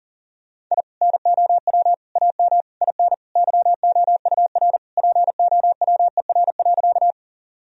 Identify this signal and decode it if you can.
Morse code, heard as a keyed tone, about 40 wpm, 700 Hz. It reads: I NOW AM IN YOUR POWER.
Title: Sentences from Top 1000 Words - Review Part 2 - 40wpm